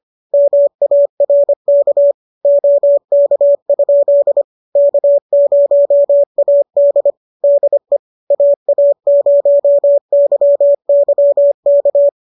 Morse code, heard as a keyed tone, about 25 wpm, 590 Hz. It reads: MARK OK? K0AD DE AA0YY K
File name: qso213 25